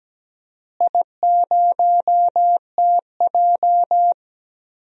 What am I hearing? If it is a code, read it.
Morse code, heard as a keyed tone, about 17 wpm, 700 Hz. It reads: I0TJ
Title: DXCC-100 Call Signs - 4 Characters - Repeated 3x - 17wpm